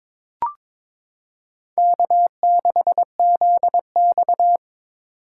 Morse code, heard as a keyed tone, about 22 wpm, 700 Hz. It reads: K6ZX
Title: K1USN SST QSO Practice - 22wpm